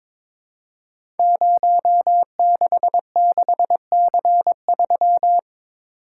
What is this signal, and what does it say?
Morse code, heard as a keyed tone, about 22 wpm, 700 Hz. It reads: 066C3
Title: Code Groups - 5 Letter:Numbers - 22wpm